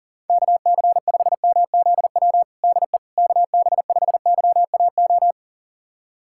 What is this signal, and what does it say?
Morse code, heard as a keyed tone, about 37 wpm, 700 Hz. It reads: KC5MZW DE KB5YAO
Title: USA Callsign DE Callsign - 6 Characters - 40wpm